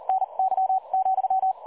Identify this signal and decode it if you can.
Morse code, heard as a keyed tone, about 40 wpm, spaced as for 32 wpm, 760 Hz, with radio noise. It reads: NY,
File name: dlazaawansowanych / nagraniazpasma / websdr recording 3565.0kHz